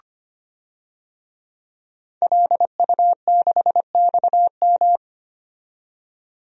Morse code, heard as a keyed tone, about 25 wpm, 700 Hz. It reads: LU6XM
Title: DXCC-100 Call Signs - 5 Characters - 25wpm